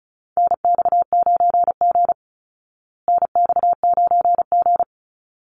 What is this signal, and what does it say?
Morse code, heard as a keyed tone, about 34 wpm, 700 Hz. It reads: NX9G NX9G
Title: Call Signs Repeated 3x - 35wpm